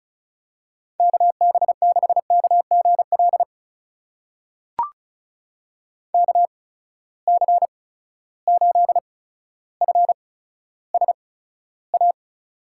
Morse code, heard as a keyed tone, about 36 wpm, spaced as for 15 wpm, 700 Hz. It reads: KB6KGL KC8FSA
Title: USA Call Signs - 6 Characters - 8x Character Spacing - 35wpm